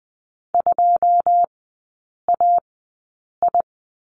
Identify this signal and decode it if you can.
Morse code, heard as a keyed tone, about 20 wpm, spaced as for 8 wpm, 690 Hz. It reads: 2AI
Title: 081 Lesson 14e - Number 2 - Three Characters 20wpm